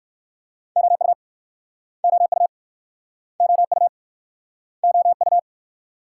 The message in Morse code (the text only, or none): OU OU OU OU